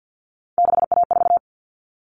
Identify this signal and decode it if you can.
Morse code, heard as a keyed tone, about 50 wpm, 700 Hz. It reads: 6A4
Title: All Cut Numbers - 3-Digit 50wpm